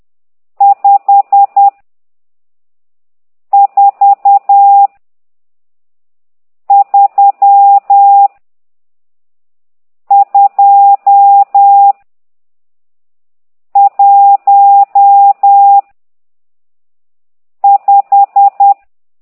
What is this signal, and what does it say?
Morse code, heard as a keyed tone, about 10 wpm, 800 Hz. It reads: 5 4 3 2 1 5
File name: ct2idw / exerc12